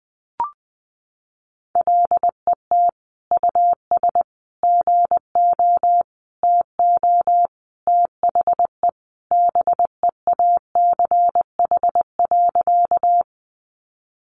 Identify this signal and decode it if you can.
Morse code, heard as a keyed tone, about 19 wpm, 700 Hz. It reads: LET US GO TO THE BEACH.
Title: Sentences from Top 1500 Words 20wpm